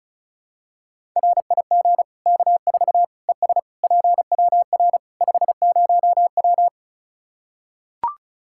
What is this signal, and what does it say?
Morse code, heard as a keyed tone, about 36 wpm, 700 Hz. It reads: RIG K4 ES PWR 50W
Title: Practice QSO - Exchange 2 - Info - 35wpm